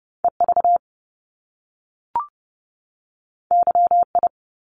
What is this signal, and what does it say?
Morse code, heard as a keyed tone, about 30 wpm, 700 Hz. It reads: EV YI
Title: Top Two Letter Combinations in Words 30wpm